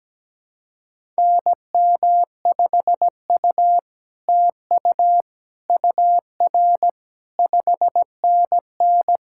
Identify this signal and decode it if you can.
Morse code, heard as a keyed tone, about 17 wpm, 700 Hz. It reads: NM5U TU UR 5NN
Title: SOTA QSO Practice 17wpm